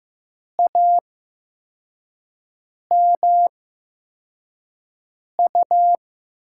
Morse code, heard as a keyed tone, about 15 wpm, spaced as for 4 wpm, 700 Hz. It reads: AMU
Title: USA Call Signs - 6 Characters - 8x Character Spacing - 15wpm